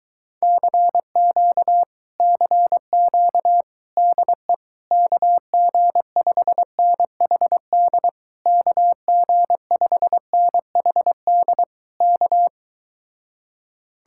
Morse code, heard as a keyed tone, about 22 wpm, 700 Hz. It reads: CQ CQ DE KG5NHD KG5NHD K